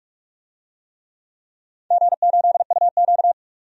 Morse code, heard as a keyed tone, about 46 wpm, 700 Hz. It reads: G8UQ
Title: DXCC-100 Call Signs - 4 Characters - 45wpm